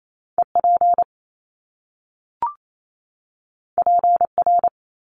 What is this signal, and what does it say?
Morse code, heard as a keyed tone, about 28 wpm, 700 Hz. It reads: EP PR
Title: Top Two Letter Combinations in Words 28wpm